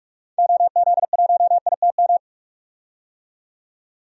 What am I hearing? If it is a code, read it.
Morse code, heard as a keyed tone, about 44 wpm, 700 Hz. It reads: OZ1ITM
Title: DXCC-100 Call Signs - 6 Characters - 45wpm